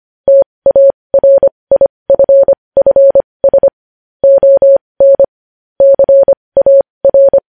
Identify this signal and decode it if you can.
Morse code, heard as a keyed tone, about 25 wpm, spaced as for 20 wpm, 550 Hz. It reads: TARIFFS ON CAR
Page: News Headlines in Morse Code at 20 WPM